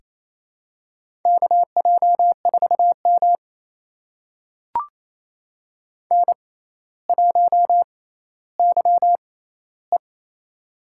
Morse code, heard as a keyed tone, about 27 wpm, spaced as for 14 wpm, 700 Hz. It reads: KJ4M N1YE
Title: USA Call Signs - 4 Characters - 6x Character Spacing - 28wpm